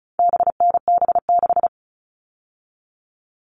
Morse code, heard as a keyed tone, about 34 wpm, 700 Hz. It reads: BNB6
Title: All Cut Numbers - 4-Digit 35wpm